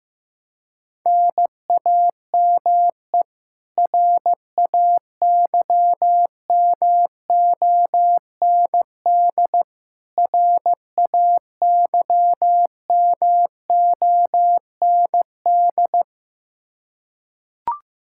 Morse code, heard as a keyed tone, about 15 wpm, 700 Hz. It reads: NAME RAYMOND RAYMOND
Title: Practice QSO - Exchange 1 - Beginning - 15wpm